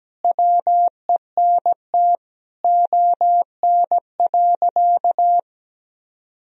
Morse code, heard as a keyed tone, about 17 wpm, 700 Hz. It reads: WENT ON.
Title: Sentences from Top 1000 Words - Review Part 1 - 17wpm